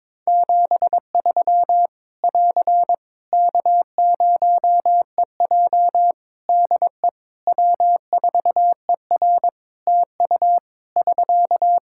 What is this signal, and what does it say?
Morse code, heard as a keyed tone, about 22 wpm, 700 Hz. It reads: 73 <AR> K0EJ DE W4ER TU <SK>
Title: Practice QSO - Exchange 4 - Ending - 22wpm